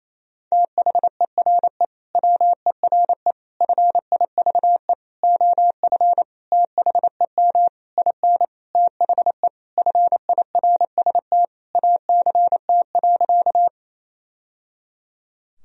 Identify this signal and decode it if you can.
Morse code, heard as a keyed tone, about 28 wpm, 700 Hz. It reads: THERE WERE FIVE OF THEM IN THE FIRST ACT.
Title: Sentences from Top 1000 Words - Review Part 2 - 28wpm